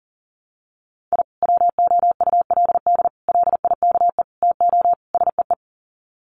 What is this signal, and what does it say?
Morse code, heard as a keyed tone, about 40 wpm, 700 Hz. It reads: I WOULD LIKE TO SEE
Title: Sets of 5 Words 40wpm